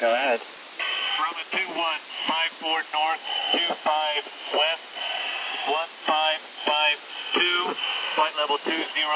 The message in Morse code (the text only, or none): none